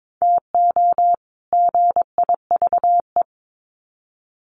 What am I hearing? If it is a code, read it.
Morse code, heard as a keyed tone, about 22 wpm, 700 Hz. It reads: TO GIVE
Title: Sets of 2 Words 22wpm